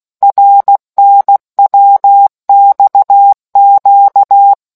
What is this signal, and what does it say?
Morse code, heard as a keyed tone, about 16 wpm, 800 Hz. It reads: RNWXQ